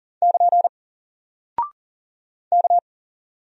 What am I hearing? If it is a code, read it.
Morse code, heard as a keyed tone, about 40 wpm, 700 Hz. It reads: <KN> K
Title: Single Character (Letter:Number:Punctuation:Prosign) Alt 40wpm